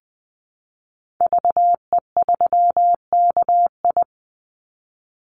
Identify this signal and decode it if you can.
Morse code, heard as a keyed tone, about 20 wpm, 700 Hz. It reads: VE3KI
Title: Call Signs - 20wpm